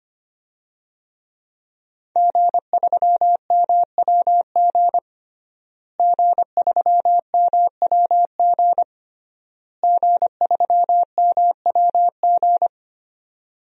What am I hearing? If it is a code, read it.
Morse code, heard as a keyed tone, about 25 wpm, 700 Hz. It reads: G3MWG G3MWG G3MWG
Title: DXCC-100 Call Signs - 5 Characters - Repeated 3x - 25wpm